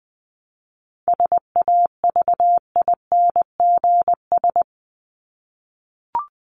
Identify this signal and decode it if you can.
Morse code, heard as a keyed tone, about 20 wpm, 700 Hz. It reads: SAVINGS
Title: Top 1001-1250 Words 20wpm